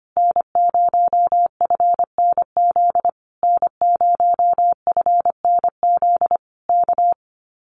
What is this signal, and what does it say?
Morse code, heard as a keyed tone, about 25 wpm, 700 Hz. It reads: N0FNZ N0FNZ K
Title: Common QSO Phrases Encore 25wpm